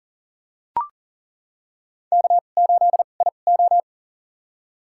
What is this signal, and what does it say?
Morse code, heard as a keyed tone, about 40 wpm, spaced as for 29 wpm, 700 Hz. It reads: K8IO
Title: USA Call Signs - 4 Characters - 2x Character Spacing - 40wpm